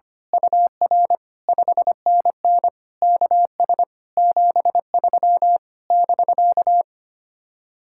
Morse code, heard as a keyed tone, about 24 wpm, 700 Hz. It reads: UR 5NN KS 73 <BK>
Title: Speed-Racing Common QSO Phrases - 25wpm